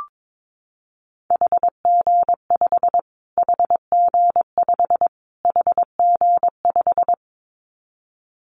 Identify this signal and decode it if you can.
Morse code, heard as a keyed tone, about 22 wpm, 700 Hz. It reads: HG5 HG5 HG5
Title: Callsign Prefixes for Top 25 DXCC - Repeated 3x - 22wpm